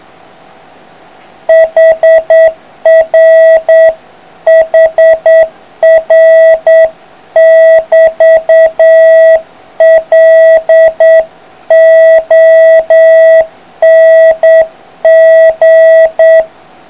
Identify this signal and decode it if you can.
Morse code, heard as a keyed tone, about 8 wpm, 650 Hz. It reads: HR HR <BT> LONG